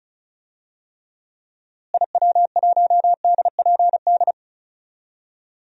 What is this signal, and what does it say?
Morse code, heard as a keyed tone, about 36 wpm, 700 Hz. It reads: IW1DPD
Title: DXCC-100 Call Signs - 6 Characters - Repeated 2x - 35wpm